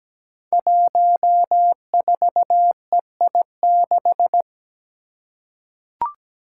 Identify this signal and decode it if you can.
Morse code, heard as a keyed tone, about 17 wpm, 700 Hz. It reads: 14EI6
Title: Five Challenging Letter-Number Combinations 17wpm